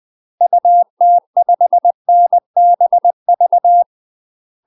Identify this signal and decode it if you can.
Morse code, heard as a keyed tone, about 20 wpm, 700 Hz. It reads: UT5NBV
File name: UK MASTER QSOs TRAINING / 20WPM / UK MASTER QSOs 0015